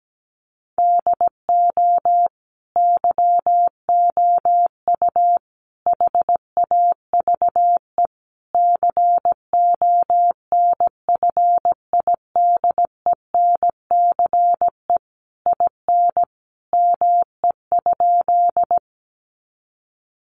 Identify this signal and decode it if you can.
Morse code, heard as a keyed tone, about 17 wpm, 700 Hz. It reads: DO YOU HAVE CONFIDENCE IN ME?
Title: Sentences from Top 1500 Words 17wpm